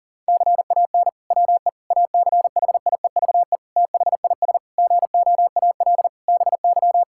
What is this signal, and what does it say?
Morse code, heard as a keyed tone, about 37 wpm, 700 Hz. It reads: CAN WE ACHIEVE THIS GOAL BY
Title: Sentences from Top 1000 Words - Review Part 5 - 40wpm